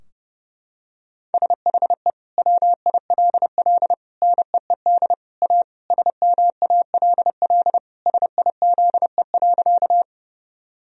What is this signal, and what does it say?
Morse code, heard as a keyed tone, about 32 wpm, 700 Hz. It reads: SHE WILL NEED A SMALL SIZE.
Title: Sentences from Top 1000 Words - Review Part 2 - 30wpm